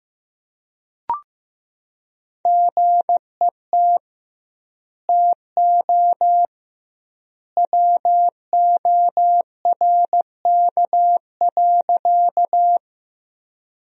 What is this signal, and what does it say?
Morse code, heard as a keyed tone, about 15 wpm, 700 Hz. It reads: GET TO WORK.
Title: 2x Word-Spacing - Sentences from Top 200 Words - 15wpm